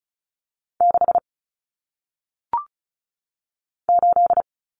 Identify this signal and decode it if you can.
Morse code, heard as a keyed tone, about 36 wpm, 700 Hz. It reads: 6 8
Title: Numbers - 1-Digit 35wpm